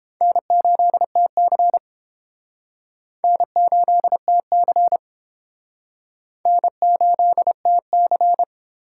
Morse code, heard as a keyed tone, about 30 wpm, 700 Hz. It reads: N8TC N8TC N8TC